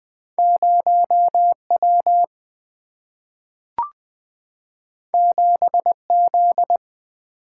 Morse code, heard as a keyed tone, about 20 wpm, 690 Hz. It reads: 0W 7Z
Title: Two Letter-Number Combinations - 20wpm